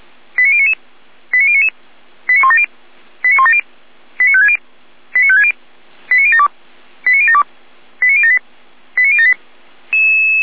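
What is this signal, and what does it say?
Five-tone selective call (ZVEI), heard as quick runs of five tones, tones of about 70 ms. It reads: 89099 89099 89170 89170 89560 89560 89072 89072 89078 89078
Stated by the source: ZVEI five-tone selective call received by SDR.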